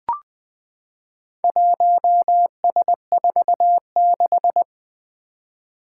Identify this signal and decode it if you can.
Morse code, heard as a keyed tone, about 20 wpm, 700 Hz. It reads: 1S46
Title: Four Challenging Letter-Number Combinations 20wpm